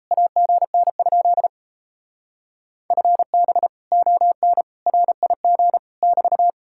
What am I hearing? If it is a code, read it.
Morse code, heard as a keyed tone, about 36 wpm, 700 Hz. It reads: AGN? FB ON RIG <BT>